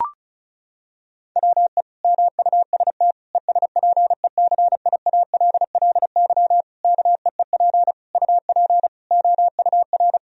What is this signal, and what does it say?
Morse code, heard as a keyed tone, about 36 wpm, 700 Hz. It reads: WE MUST ESPECIALLY KEEP UP OUR
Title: Sentences from Top 1000 Words - Review Part 4 - 35wpm